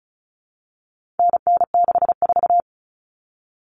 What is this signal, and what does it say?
Morse code, heard as a keyed tone, about 34 wpm, 700 Hz. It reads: NN64
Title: Common Cut Numbers - 4-Digit 35wpm